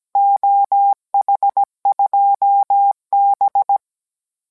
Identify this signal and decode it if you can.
Morse code, heard as a keyed tone, about 17 wpm, 800 Hz. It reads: OH2B